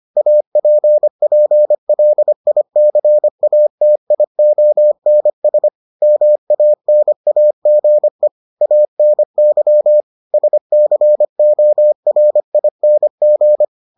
Morse code, heard as a keyed tone, about 25 wpm, 600 Hz. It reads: APPLICATIONS MANAGE ANY SCORING